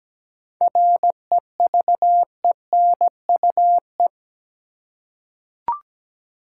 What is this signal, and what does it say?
Morse code, heard as a keyed tone, about 17 wpm, 700 Hz. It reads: REVENUE
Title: Seven Letter Words - 17wpm